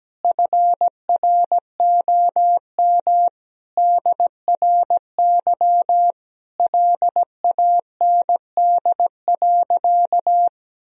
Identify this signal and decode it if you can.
Morse code, heard as a keyed tone, about 17 wpm, 700 Hz. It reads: FROM DRY LAND.
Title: Sentences from Top 1000 Words - Review Part 3 - 17wpm